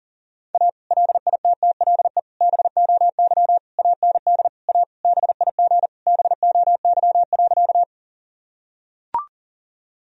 Morse code, heard as a keyed tone, about 40 wpm, 700 Hz. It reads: A LITTLE BOY AND A BIG BOY.
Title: Sentences from Top 1000 Words - Review Part 1 - 40wpm